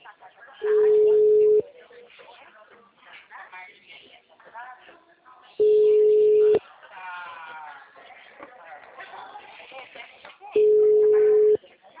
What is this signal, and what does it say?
Ringback tone (Europe, 425 Hz), with speech.
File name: out- -2008- - - .111